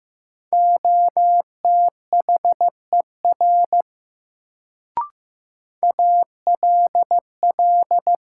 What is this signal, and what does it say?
Morse code, heard as a keyed tone, about 15 wpm, 700 Hz. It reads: OTHER ALL
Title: Sentences from Top 100 Words - New Word Drill - 15wpm